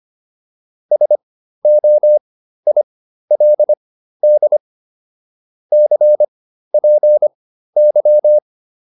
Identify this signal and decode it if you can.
Morse code, heard as a keyed tone, about 25 wpm, spaced as for 13 wpm, 600 Hz. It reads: SOILD CPY